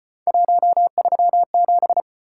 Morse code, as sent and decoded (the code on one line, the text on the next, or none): .---- ...-- --...
137